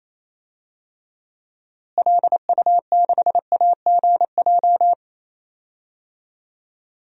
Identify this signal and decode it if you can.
Morse code, heard as a keyed tone, about 28 wpm, 700 Hz. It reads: LU6AGJ
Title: DXCC-100 Call Signs - 6 Characters - Repeated 3x - 28wpm